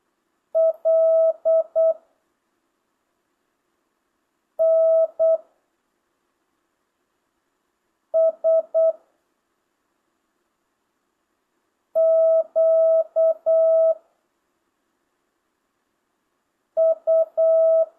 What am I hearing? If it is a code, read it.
Morse code, heard as a keyed tone, about 8 wpm, 650 Hz. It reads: L N S Q U
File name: jh8cbh / 2025cw